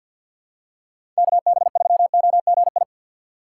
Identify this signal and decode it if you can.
Morse code, heard as a keyed tone, about 47 wpm, 700 Hz. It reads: KB3OGI
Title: USA Call Signs - 6 Characters - 3x Character Spacing - 50wpm